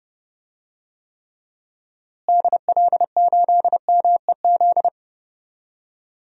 Morse code, heard as a keyed tone, about 32 wpm, 700 Hz. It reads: DL8MEZ
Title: DXCC-100 Call Signs - 6 Characters - Repeated 2x - 30wpm